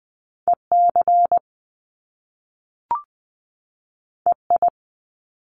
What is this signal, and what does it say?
Morse code, heard as a keyed tone, about 20 wpm, 700 Hz. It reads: EC EI
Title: Top Two Letter Combinations in Words 20wpm